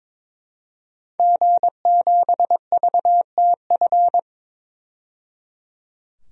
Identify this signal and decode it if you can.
Morse code, heard as a keyed tone, about 22 wpm, 700 Hz. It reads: G7VTF
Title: DXCC-100 Call Signs - 5 Characters - 22wpm